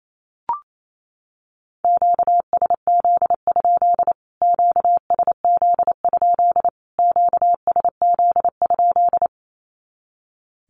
Morse code, heard as a keyed tone, about 28 wpm, 700 Hz. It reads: QSZ? QSZ? QSZ?
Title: Q-Codes Repeated 3x 28wpm